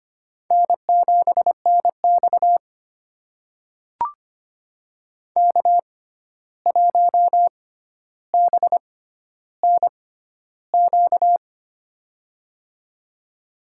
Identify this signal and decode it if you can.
Morse code, heard as a keyed tone, about 25 wpm, spaced as for 12 wpm, 700 Hz. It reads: N7NX K1BNQ
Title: USA Call Signs - 6x Character Spacing - 25wpm